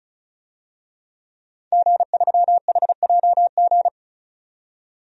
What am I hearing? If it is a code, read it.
Morse code, heard as a keyed tone, about 36 wpm, 700 Hz. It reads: G3HJG
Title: DXCC-100 Call Signs - 5 Characters - 35wpm